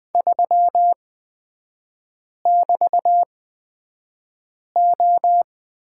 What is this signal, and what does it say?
Morse code, heard as a keyed tone, about 20 wpm, spaced as for 5 wpm, 700 Hz. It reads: 3<BT>O
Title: 224 Lesson 36e - BT - Three Characters 20wpm